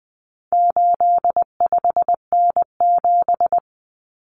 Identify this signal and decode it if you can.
Morse code, heard as a keyed tone, about 20 wpm, 700 Hz. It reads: 85N7
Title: Common Cut Numbers - 4-Digit 20wpm